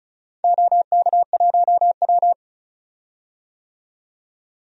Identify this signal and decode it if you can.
Morse code, heard as a keyed tone, about 36 wpm, 700 Hz. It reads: OK1W
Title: DXCC-100 Call Signs - 4 Characters - 35wpm